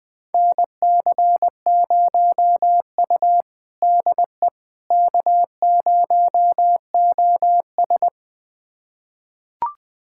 Morse code, heard as a keyed tone, about 20 wpm, 700 Hz. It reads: NC0U DE K0OS
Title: USA Callsign DE Callsign - 4 Characters - 20wpm